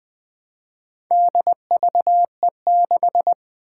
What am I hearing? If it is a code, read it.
Morse code, heard as a keyed tone, about 20 wpm, 700 Hz. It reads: DVE6
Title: Four Challenging Letter-Number Combinations 20wpm